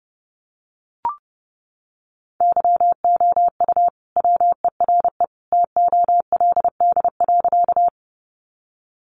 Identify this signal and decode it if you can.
Morse code, heard as a keyed tone, about 30 wpm, 700 Hz. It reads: YOU WERE TOLD.
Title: Sentences from Top 400 Words 30wpm